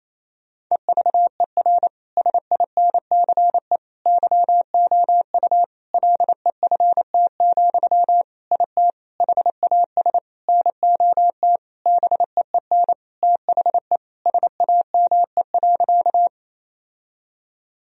Morse code, heard as a keyed tone, about 28 wpm, 700 Hz. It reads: EVER SINCE YOU LEFT, IT HAS NOT BEEN THE SAME.
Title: Sentences from Top 1000 Words - Review Part 1 - 28wpm